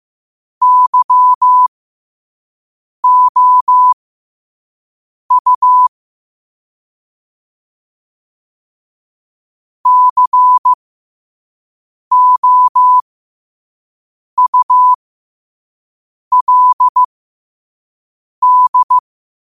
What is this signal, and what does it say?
Morse code, heard as a keyed tone, about 15 wpm, spaced as for 5 wpm, 1000 Hz. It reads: YOU COULD
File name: qotdfiles / 5wpm